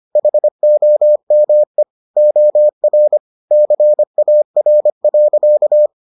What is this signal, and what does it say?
Morse code, heard as a keyed tone, about 25 wpm, 600 Hz. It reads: HOME OR CAR.